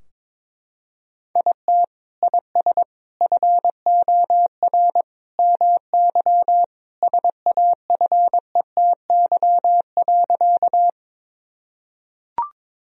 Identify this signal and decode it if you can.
Morse code, heard as a keyed tone, about 22 wpm, 700 Hz. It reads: IT IS FOR MY SAFETY.
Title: Sentences from Top 1000 Words - Review Part 4 - 22wpm